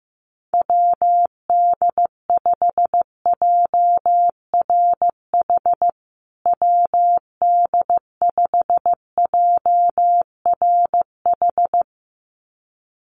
Morse code, heard as a keyed tone, about 15 wpm, 700 Hz. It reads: WD5JRH WD5JRH
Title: USA Call Signs Repeated 2x 15wpm